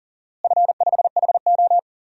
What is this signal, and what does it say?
Morse code, heard as a keyed tone, about 40 wpm, 700 Hz. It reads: F5HO
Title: DXCC-100 Call Signs - 4 Characters - Repeated 3x - 40wpm